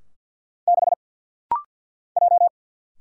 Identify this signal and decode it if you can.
Morse code, heard as a keyed tone, about 50 wpm, 700 Hz. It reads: 6 J
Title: Single Letter-Number Alt - Rapid-Fire 50wpm